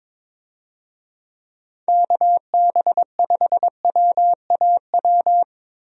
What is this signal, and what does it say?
Morse code, heard as a keyed tone, about 22 wpm, 700 Hz. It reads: KB5WAW
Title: USA Call Signs - 6 Characters - Repeated 3x - 22wpm